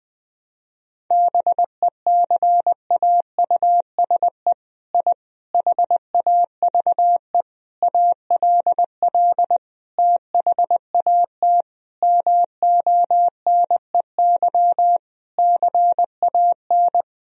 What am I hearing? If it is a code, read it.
Morse code, heard as a keyed tone, about 20 wpm, 700 Hz. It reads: BECAUSE I HAVE ALL THAT MONEY CAN
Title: Sentences from Top 1000 Words - Review Part 4 - 20wpm